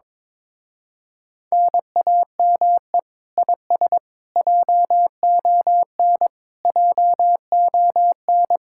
Morse code, heard as a keyed tone, about 22 wpm, 700 Hz. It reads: NAME IS JON JON
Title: Speed-Racing Common QSO Phrases - 22wpm